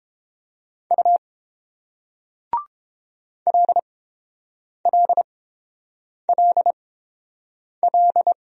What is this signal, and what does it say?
Morse code, heard as a keyed tone, about 30 wpm, spaced as for 7 wpm, 700 Hz. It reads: U LLLL